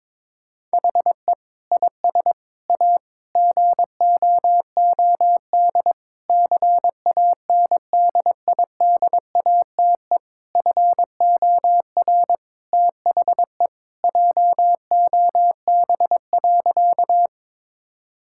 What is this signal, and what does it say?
Morse code, heard as a keyed tone, about 22 wpm, 700 Hz. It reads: HE IS A GOOD CANDIDATE FOR THE JOB.
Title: Sentences from Top 1600 Words 22wpm